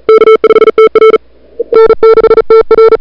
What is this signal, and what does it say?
Morse code, heard as a keyed tone, about 37 wpm, 440 Hz. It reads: K5TR N6TR
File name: n6tr / Fab2007CWSprintHour1